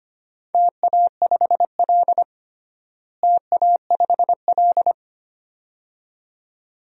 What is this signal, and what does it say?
Morse code, heard as a keyed tone, about 25 wpm, 700 Hz. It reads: TA5L TA5L
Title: DXCC-100 Call Signs - 4 Characters - Repeated 2x - 25wpm